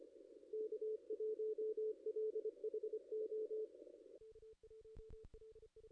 Morse code, heard as a keyed tone, about 25 wpm, 430 Hz. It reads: K1LHO
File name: contests / ss-cw-k5tr-op-n5rz -2200